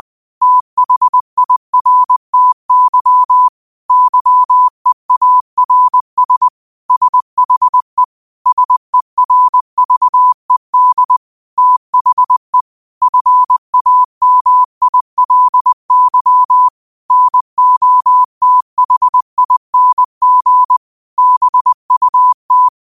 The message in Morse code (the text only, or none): THIRTY YEARS SHE SERVED THE FAMILY NOTHING BUT